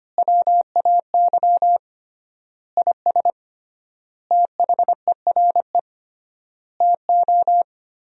Morse code, heard as a keyed tone, about 25 wpm, 700 Hz. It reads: WAY IS THERE TO
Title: 3x Word-Spacing - Sentences from Top 100 Words - 25wpm